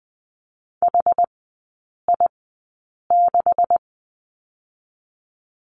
Morse code, heard as a keyed tone, about 20 wpm, spaced as for 8 wpm, 700 Hz. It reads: HI6
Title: 138 Lesson 23e - Letter f - Three Characters 20wpm